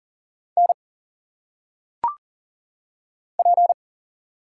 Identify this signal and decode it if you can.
Morse code, heard as a keyed tone, about 40 wpm, 700 Hz. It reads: N P
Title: Single Letters - Phonetic Pronunciation 40wpm